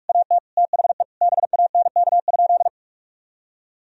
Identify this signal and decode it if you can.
Morse code, heard as a keyed tone, about 41 wpm, 700 Hz. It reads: AT THE BANK?
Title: Sentences from Top 1000 Words - Review Part 4 - 45wpm